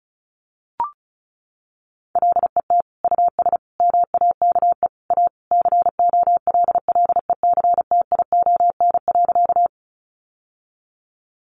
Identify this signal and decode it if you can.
Morse code, heard as a keyed tone, about 34 wpm, 700 Hz. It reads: LET US MAKE A COLLECTION.
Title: Sentences from Top 1200 Words 35wpm